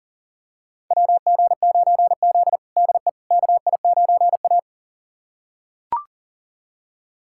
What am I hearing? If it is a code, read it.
Morse code, heard as a keyed tone, about 37 wpm, 700 Hz. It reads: WG9Z DE KI9A
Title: USA Callsign DE Callsign - 4 Characters - 40wpm